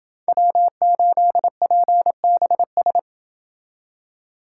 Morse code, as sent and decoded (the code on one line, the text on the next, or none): .-- ---.. .--. -... ...
W8PBS